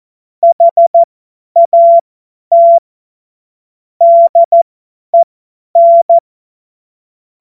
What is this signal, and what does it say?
Morse code, heard as a keyed tone, about 14 wpm, spaced as for 10 wpm, 680 Hz. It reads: HAT DEN
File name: Morsekurs / Witze / Witz 30. Dezember 2025 50 F70